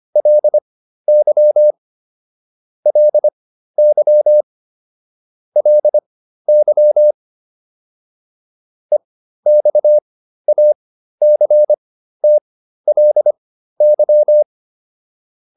Morse code, heard as a keyed tone, about 25 wpm, spaced as for 13 wpm, 600 Hz. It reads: LY LY LY EXACTLY
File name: LY7 13